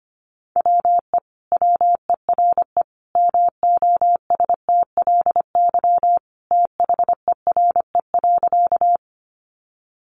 Morse code, heard as a keyed tone, about 25 wpm, 700 Hz. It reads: WE WERE MOSTLY THERE.
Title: Sentences from Top 1300 Words 25wpm